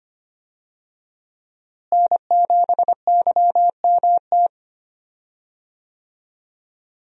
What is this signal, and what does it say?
Morse code, heard as a keyed tone, about 25 wpm, 700 Hz. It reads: N7YMT
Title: USA Call Signs - 5 Characters - Repeated 2x - 25wpm